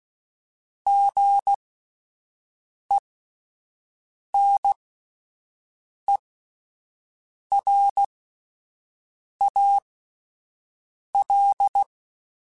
Morse code, heard as a keyed tone, about 16 wpm, spaced as for 5 wpm, 800 Hz. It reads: GENERAL
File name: fromSMs / FromSMs mp3s / Classes@5